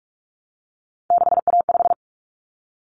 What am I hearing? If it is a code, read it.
Morse code, heard as a keyed tone, about 44 wpm, 700 Hz. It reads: 6A5
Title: Common Cut Numbers - 3-Digit 45wpm